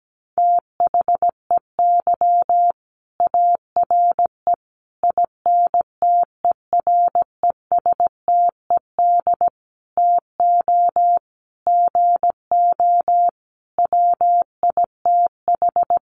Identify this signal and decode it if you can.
Morse code, heard as a keyed tone, about 17 wpm, 700 Hz. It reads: THEY ARE INTERESTED TO GO WITH
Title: Sentences from Top 900 Words 17wpm